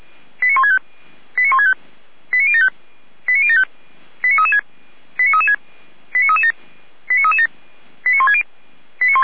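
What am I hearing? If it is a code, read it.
Five-tone selective call (ZVEI), heard as quick runs of five tones, tones of about 70 ms. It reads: 89165 89165 89075 89075 89336 89336 89337 89337 89160
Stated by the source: ZVEI five-tone selective call received by SDR.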